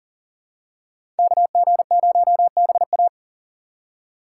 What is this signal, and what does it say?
Morse code, heard as a keyed tone, about 40 wpm, 700 Hz. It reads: KG0BA
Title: USA Call Signs - 5 Characters - 3x Character Spacing - 40wpm